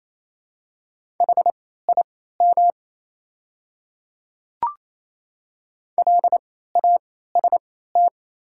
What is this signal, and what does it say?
Morse code, heard as a keyed tone, about 28 wpm, spaced as for 16 wpm, 700 Hz. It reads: HIM LAST
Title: 3x Character Spacing - Top 500 Words - 28wpm